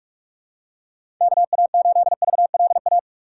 Morse code, heard as a keyed tone, about 43 wpm, 700 Hz. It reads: KA8VLA
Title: USA Call Signs - 6 Characters - 6x Character Spacing - 45wpm